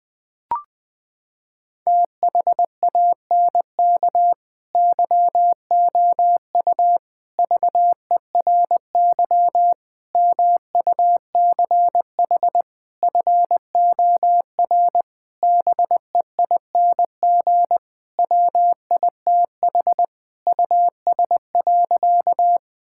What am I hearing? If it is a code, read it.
Morse code, heard as a keyed tone, about 21 wpm, 700 Hz. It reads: THANK YOU VERY MUCH FOR BEING WITH US.
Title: Sentences with 5-Word Phrases - 20wpm